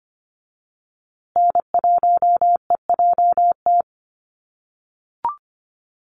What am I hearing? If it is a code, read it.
Morse code, heard as a keyed tone, about 25 wpm, 700 Hz. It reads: N1EJT
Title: USA Call Signs Repeated 2x 25wpm